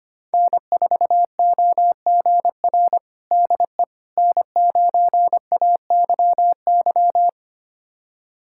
Morse code, heard as a keyed tone, about 25 wpm, 700 Hz. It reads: N4OGR DE N9AYY
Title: USA Callsign DE Callsign - 5 Characters - 25wpm